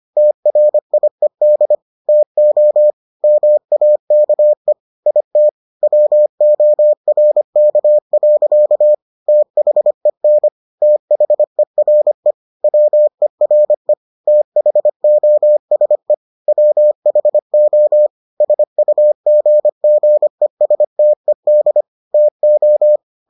Morse code, heard as a keyed tone, about 25 wpm, 600 Hz. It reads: TRIED TO MAKE IT WORK. THEN THERE WERE THOSE WHO SUGGESTED TO